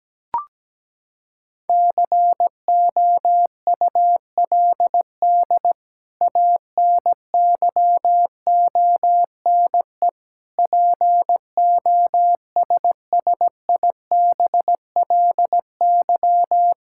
Morse code, heard as a keyed tone, about 17 wpm, 700 Hz. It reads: COULD ANYONE POSSIBLY
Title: Sentences from Top 1600 Words 17wpm